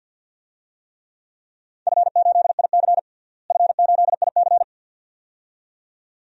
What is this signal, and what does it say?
Morse code, heard as a keyed tone, about 52 wpm, 700 Hz. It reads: F8IC F8IC
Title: DXCC-100 Call Signs - 4 Characters - Repeated 2x - 50wpm